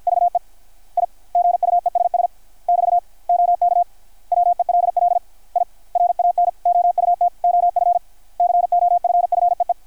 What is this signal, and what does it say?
Morse code, heard as a keyed tone, about 52 wpm, 700 Hz. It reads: VE I GUESS <BT> OK WELL I RAN OUT OF COFFEE
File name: archived / CRSnet